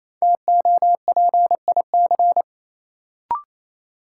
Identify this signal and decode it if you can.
Morse code, heard as a keyed tone, about 28 wpm, 700 Hz. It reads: TOPIC
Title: Sentences from Top 1200 Words - New Word Drill Alt - 28wpm